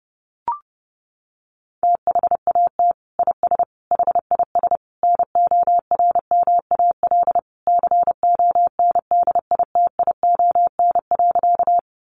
Morse code, heard as a keyed tone, about 30 wpm, 700 Hz. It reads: THAT IS HIS NORMAL CONDITION.
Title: Sentences from Top 1200 Words 30wpm